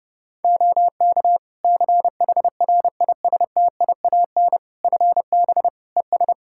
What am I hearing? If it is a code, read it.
Morse code, heard as a keyed tone, about 30 wpm, 700 Hz. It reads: OK CHRISTIAN FB ES
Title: Practice QSO - Exchange 2 - Info - 30wpm